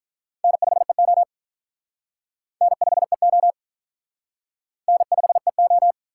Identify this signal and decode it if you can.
Morse code, heard as a keyed tone, about 47 wpm, 700 Hz. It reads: N5EO N5EO N5EO